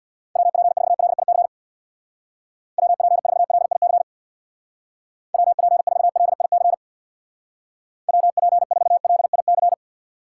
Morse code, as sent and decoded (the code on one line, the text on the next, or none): .-- .--. ....- .-.. .. -.-. / .-- .--. ....- .-.. .. -.-. / .-- .--. ....- .-.. .. -.-. / .-- .--. ....- .-.. .. -.-.
WP4LIC WP4LIC WP4LIC WP4LIC